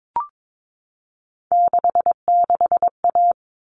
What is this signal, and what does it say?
Morse code, heard as a keyed tone, about 22 wpm, 700 Hz. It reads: 66A
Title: All Cut Numbers - 3-Digit 22wpm